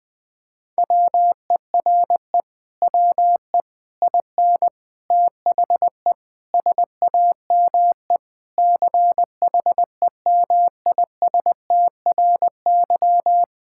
Morse code, heard as a keyed tone, about 20 wpm, 700 Hz. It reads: WERE WE IN THE SAME CHEMISTRY
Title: Sentences from Top 1000 Words - Review Part 6 - 20wpm